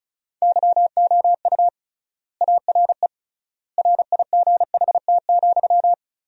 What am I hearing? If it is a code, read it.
Morse code, heard as a keyed tone, about 36 wpm, 700 Hz. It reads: YOU ARE RIGHT,
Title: 3x Word-Spacing - Sentences from Top 300 Words - 35wpm